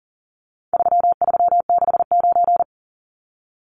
Morse code, heard as a keyed tone, about 40 wpm, 700 Hz. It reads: 3369
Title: Numbers - 4-Digit 40wpm